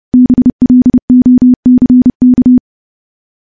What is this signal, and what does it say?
Morse code, heard as a keyed tone, about 30 wpm, 250 Hz. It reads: BLOCK
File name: morse / morse-training-30WPM